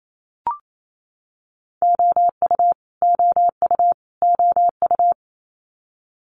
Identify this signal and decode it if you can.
Morse code, heard as a keyed tone, about 28 wpm, 700 Hz. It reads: OU OU OU
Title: Multi-letter Phonemes Repeated 3x 28wpm